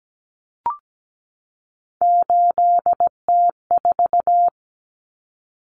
Morse code, heard as a keyed tone, about 17 wpm, 700 Hz. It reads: 8T4
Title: Common Cut Numbers - 3-Digit 17wpm